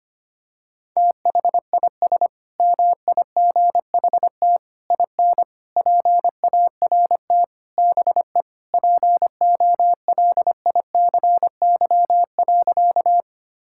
Morse code, heard as a keyed tone, about 25 wpm, 700 Hz. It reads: THIS MIGHT IN PART BE POLICY.
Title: Sentences from Top 1000 Words - Review Part 5 - 25wpm